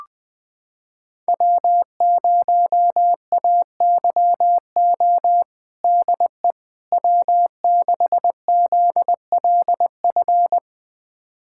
Morse code, heard as a keyed tone, about 21 wpm, 700 Hz. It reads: W0AYO DE W6ZLF
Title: USA Callsign DE Callsign - 5 Characters - 20wpm